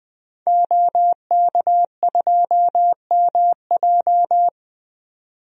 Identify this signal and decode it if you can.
Morse code, heard as a keyed tone, about 20 wpm, 700 Hz. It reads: OK2MJ
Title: DXCC-100 Call Signs - 5 Characters - 20wpm